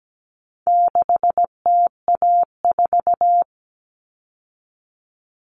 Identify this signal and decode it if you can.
Morse code, heard as a keyed tone, about 17 wpm, 700 Hz. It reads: 6TA4
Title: All Cut Numbers - 4-Digit 17wpm